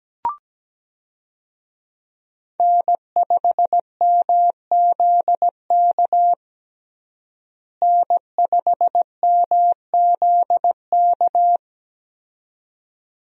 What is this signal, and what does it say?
Morse code, heard as a keyed tone, about 17 wpm, 700 Hz. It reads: N5MZK N5MZK
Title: USA Call Signs - 5 Characters - Repeated 2x - 17wpm